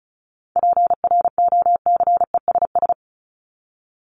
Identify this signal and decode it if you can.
Morse code, heard as a keyed tone, about 34 wpm, 700 Hz. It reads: PROCESS
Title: ING Suffix - Top 100 Words with Base Words 35wpm